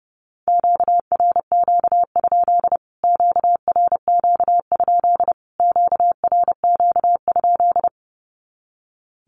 Morse code, heard as a keyed tone, about 30 wpm, 700 Hz. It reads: QRQ? QRQ? QRQ?
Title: Q-Codes Repeated 3x 30wpm